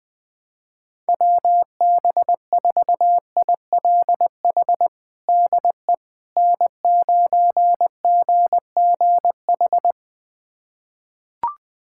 Morse code, heard as a keyed tone, about 20 wpm, 700 Hz. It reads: WB4ILH DE N9GGH
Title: USA Callsign DE Callsign 20wpm